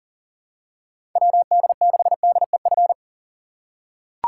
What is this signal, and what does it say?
Morse code, heard as a keyed tone, about 40 wpm, 700 Hz. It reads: WD6DEF
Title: USA Call Signs - 6 Characters - 4x Character Spacing - 40wpm